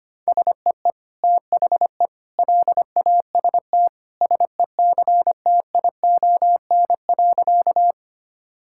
Morse code, heard as a keyed tone, about 25 wpm, 700 Hz. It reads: SEE THE LAST SECTION.
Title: Sentences from Top 1000 Words - Review Part 5 - 25wpm